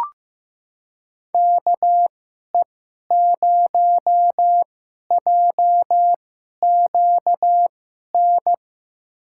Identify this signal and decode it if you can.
Morse code, heard as a keyed tone, about 15 wpm, spaced as for 11 wpm, 700 Hz. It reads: KE0JQN
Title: USA Call Signs - 6 Characters - 2x Character Spacing - 15wpm